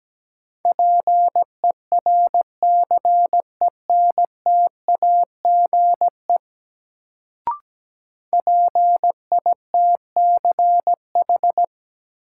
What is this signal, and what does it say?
Morse code, heard as a keyed tone, about 17 wpm, 700 Hz. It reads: PERCENTAGE PITCH
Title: Sentences from Top 1300 Words - New Word Drill Alt - 17wpm